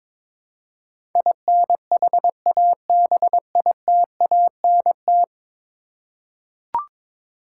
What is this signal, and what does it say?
Morse code, heard as a keyed tone, about 23 wpm, 700 Hz. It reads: INHABITANT
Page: morse-code-ninja-practice-22wpm